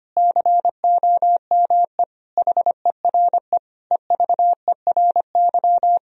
Morse code, heard as a keyed tone, about 25 wpm, 700 Hz. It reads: COME HERE EVERY
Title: Sentences from Top 1000 Words - Review Part 1 - 25wpm